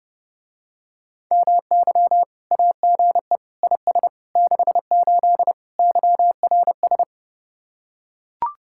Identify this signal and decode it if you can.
Morse code, heard as a keyed tone, about 29 wpm, 700 Hz. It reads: MY AGE IS 68 YRS